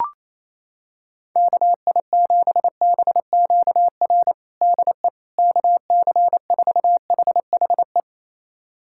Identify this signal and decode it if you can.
Morse code, heard as a keyed tone, about 27 wpm, 700 Hz. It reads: KI7BQR DE KC4HHE
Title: USA Callsign DE Callsign 28wpm